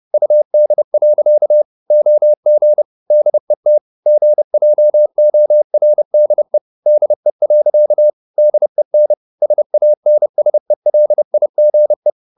Morse code, heard as a keyed tone, about 30 wpm, 600 Hz. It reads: UD. OG DET GJORDE DE. DEN SANSELIGE
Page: Ferie 30 wpm